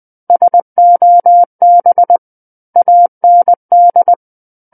Morse code, heard as a keyed tone, about 20 wpm, 700 Hz. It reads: SOB AND
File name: 20 WPM